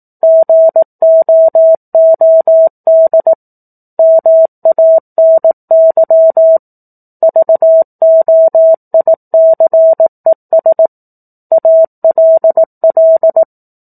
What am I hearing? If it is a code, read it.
Morse code, heard as a keyed tone, about 18 wpm, 650 Hz. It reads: GOOD MANY VOICES ALL